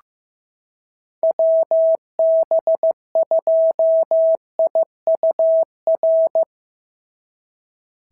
Morse code, heard as a keyed tone, about 15 wpm, 650 Hz. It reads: WB2IUR
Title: SOTA QSO Practice 15wpm